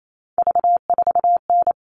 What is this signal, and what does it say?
Morse code, heard as a keyed tone, about 28 wpm, 700 Hz. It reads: V4N
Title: All Cut Numbers - 3-Digit 28wpm